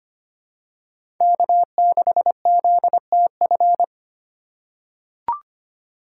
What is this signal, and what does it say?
Morse code, heard as a keyed tone, about 25 wpm, 700 Hz. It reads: K6ZTF
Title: USA Call Signs - 5 Characters - 2x Character Spacing - 25wpm